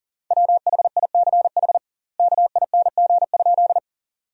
Morse code, heard as a keyed tone, about 40 wpm, 700 Hz. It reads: WHICH KING?
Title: 2x Word-Spacing - Sentences from Top 400 Words - 40wpm